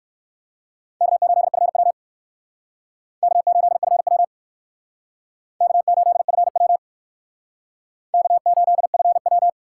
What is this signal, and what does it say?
Morse code, heard as a keyed tone, about 65 wpm, 700 Hz. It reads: K8FW K8FW K8FW K8FW